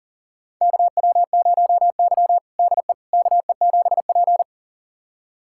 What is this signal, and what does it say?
Morse code, heard as a keyed tone, about 40 wpm, 700 Hz. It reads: KW0Y DE KE7P
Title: USA Callsign DE Callsign - 4 Characters - 40wpm